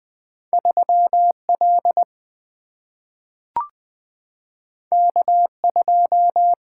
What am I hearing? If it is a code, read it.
Morse code, heard as a keyed tone, about 20 wpm, 700 Hz. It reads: 3L K2
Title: Single-Letter Single-Number 20wpm